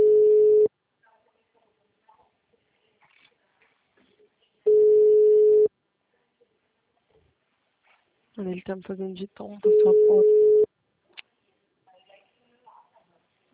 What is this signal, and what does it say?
Ringback tone (Europe, 425 Hz), with speech.
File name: out- -2006- - - .28271